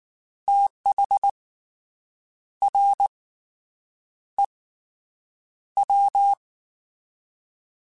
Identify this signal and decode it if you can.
Morse code, heard as a keyed tone, about 19 wpm, 800 Hz. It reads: TH R E W
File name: Fast TH @18